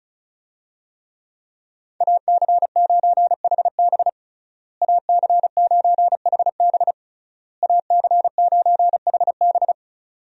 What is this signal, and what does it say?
Morse code, heard as a keyed tone, about 36 wpm, 700 Hz. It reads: AC9HB AC9HB AC9HB
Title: USA Call Signs - 5 Characters - Repeated 3x - 35wpm